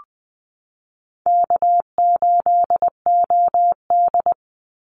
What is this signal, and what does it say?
Morse code, heard as a keyed tone, about 20 wpm, 700 Hz. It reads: K8OD
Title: Easy Call Signs Repeated 3x - 20wpm